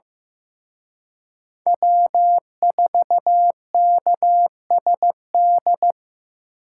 Morse code, heard as a keyed tone, about 18 wpm, 700 Hz. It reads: W4KSD
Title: Speed Racing - USA 1x3 Call Signs 15wpm